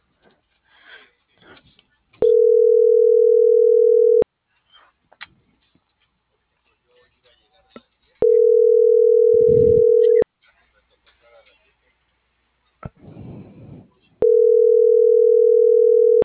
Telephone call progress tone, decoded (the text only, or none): ringback tone